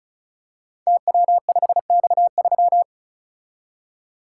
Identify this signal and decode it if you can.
Morse code, heard as a keyed tone, about 36 wpm, 700 Hz. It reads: TW5X3
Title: Code Groups - 5 Letter:Numbers - 35wpm